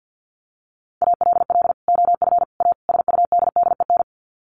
Morse code, heard as a keyed tone, about 48 wpm, 700 Hz. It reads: ALL OF A SUDDEN
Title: Sets of 4 Words 50wpm